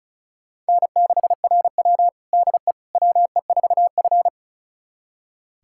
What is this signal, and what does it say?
Morse code, heard as a keyed tone, about 36 wpm, 700 Hz. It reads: N6RW DE WE4F
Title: USA Callsign DE Callsign - 4 Characters - 35wpm